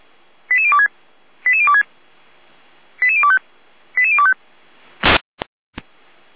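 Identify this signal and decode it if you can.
Five-tone selective call (ZVEI), heard as quick runs of five tones, tones of about 70 ms. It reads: 88026 88026 88025 88025
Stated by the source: ZVEI five-tone selective call received by SDR, from the Feuerwehr Orlamünde fire brigade audio server.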